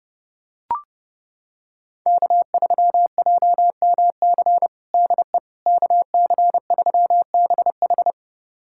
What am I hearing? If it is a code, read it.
Morse code, heard as a keyed tone, about 30 wpm, 700 Hz. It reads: K3JMC DE KC3BH
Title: USA Callsign DE Callsign - 5 Characters - 30wpm